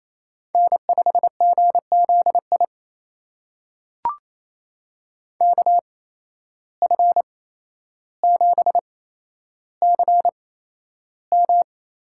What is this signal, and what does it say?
Morse code, heard as a keyed tone, about 28 wpm, spaced as for 12 wpm, 700 Hz. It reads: N5GZI KF7CM
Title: USA Call Signs - 5 Characters - 8x Character Spacing - 28wpm